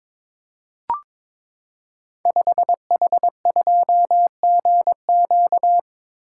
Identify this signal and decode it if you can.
Morse code, heard as a keyed tone, about 22 wpm, 700 Hz. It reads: 5H2GQ
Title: Code Groups - 5 Letter:Numbers - 22wpm